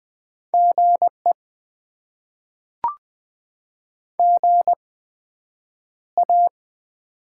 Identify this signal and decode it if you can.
Morse code, heard as a keyed tone, about 20 wpm, spaced as for 9 wpm, 700 Hz. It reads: GE GA
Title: Top 20 QSO Elements - 8x Character Spacing - 20wpm